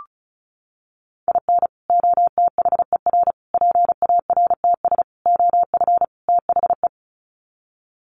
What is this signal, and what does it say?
Morse code, heard as a keyed tone, about 34 wpm, 700 Hz. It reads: IN OTHER PARTS OF THE
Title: Sets of 5 Words 35wpm